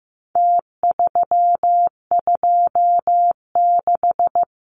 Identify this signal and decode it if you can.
Morse code, heard as a keyed tone, about 15 wpm, 700 Hz. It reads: T326
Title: Numbers - Contest Style - 4-Digit 15wpm